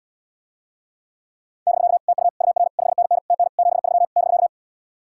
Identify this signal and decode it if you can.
Morse code, heard as a keyed tone, about 77 wpm, 700 Hz. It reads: <BK> TU UR 5NN TN 73 <BK>